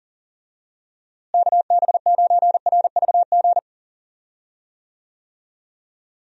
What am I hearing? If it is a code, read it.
Morse code, heard as a keyed tone, about 40 wpm, 700 Hz. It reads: KB9RVG
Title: USA Call Signs - 6 Characters - Repeated 2x - 40wpm